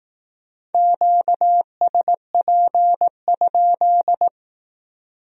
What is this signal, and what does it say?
Morse code, heard as a keyed tone, about 18 wpm, 700 Hz. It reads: QSP?
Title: Speed-Racing Q-Codes - 15wpm